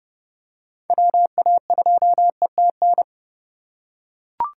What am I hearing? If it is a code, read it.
Morse code, heard as a keyed tone, about 28 wpm, 700 Hz. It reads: WA2ETN